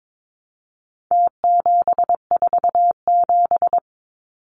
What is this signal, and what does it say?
Morse code, heard as a keyed tone, about 22 wpm, 700 Hz. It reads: T747
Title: Numbers - Contest Style - 4-Digit 22wpm